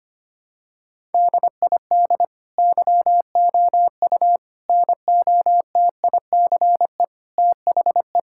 Morse code, heard as a keyed tone, about 25 wpm, 700 Hz. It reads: DID YOU NOTICE THE
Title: Sentences from Top 1000 Words - Review Part 5 - 25wpm